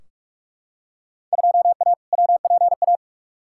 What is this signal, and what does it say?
Morse code, heard as a keyed tone, about 46 wpm, 700 Hz. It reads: 2A WPA
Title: Field Day Exchanges - 45wpm